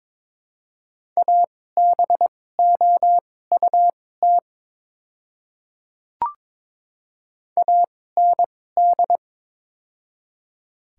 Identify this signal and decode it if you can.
Morse code, heard as a keyed tone, about 22 wpm, spaced as for 16 wpm, 700 Hz. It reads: ABOUT AND
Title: 2x Character Spacing - Top 100 Words - 22wpm